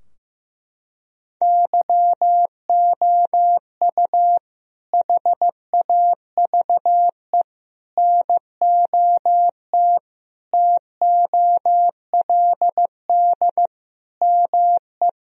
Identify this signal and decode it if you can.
Morse code, heard as a keyed tone, about 16 wpm, 700 Hz. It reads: YOU HAVE NOT TOLD ME
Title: Sentences from Top 1000 Words - Review Part 2 - 15wpm